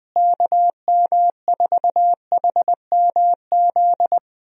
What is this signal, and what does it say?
Morse code, heard as a keyed tone, about 20 wpm, 700 Hz. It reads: KM4HMZ
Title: USA Call Signs - 6 Characters - Repeated 3x - 20wpm